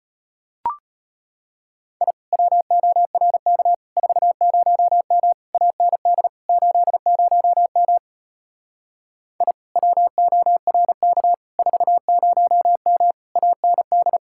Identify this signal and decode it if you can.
Morse code, heard as a keyed tone, about 33 wpm, 700 Hz. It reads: I WORK 40M AND 80M I WORK 40M AND